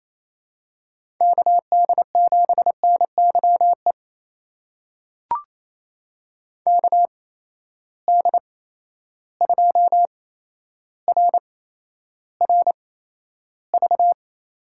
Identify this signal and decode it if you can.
Morse code, heard as a keyed tone, about 28 wpm, spaced as for 12 wpm, 700 Hz. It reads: KD7NYE KD2RRV
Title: USA Call Signs - 6 Characters - 8x Character Spacing - 28wpm